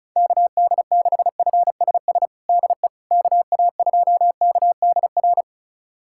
Morse code, heard as a keyed tone, about 34 wpm, 700 Hz. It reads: KD6FSS DE KA2KDR
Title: USA Callsign DE Callsign 35wpm